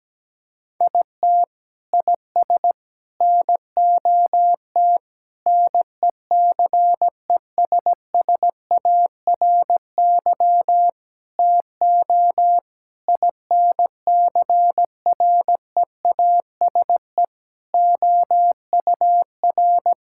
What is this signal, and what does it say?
Morse code, heard as a keyed tone, about 17 wpm, 700 Hz. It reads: IT IS NOT NECESSARY TO INCREASE OUR
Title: Sentences from Top 1000 Words - Review Part 6 - 17wpm